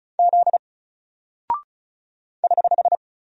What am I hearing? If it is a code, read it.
Morse code, heard as a keyed tone, about 36 wpm, 700 Hz. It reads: Z <HH>
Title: Single Character (Letter:Number:Punctuation:Prosign) 35wpm